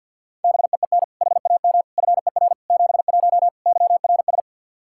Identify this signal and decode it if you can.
Morse code, heard as a keyed tone, about 53 wpm, 700 Hz. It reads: BEEN HAM FER 71 YRS
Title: Practice QSO - Exchange 3 - More Info - 50wpm